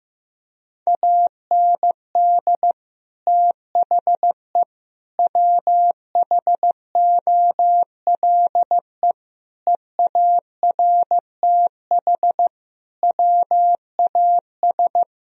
Morse code, heard as a keyed tone, about 15 wpm, 700 Hz. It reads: AND THE WHOLE EARTH WAS
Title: Sentences from Top 1000 Words - Review Part 3 - 15wpm